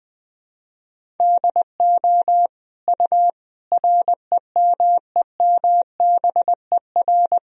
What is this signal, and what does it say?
Morse code, heard as a keyed tone, about 20 wpm, 700 Hz. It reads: DO U REMEMBER
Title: Sentences from Top 400 Words and Abbreviations 20wpm